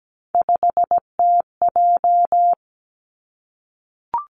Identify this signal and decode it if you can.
Morse code, heard as a keyed tone, about 17 wpm, 700 Hz. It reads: 5TJ
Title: Three Challenging Letter-Number Combinations 17wpm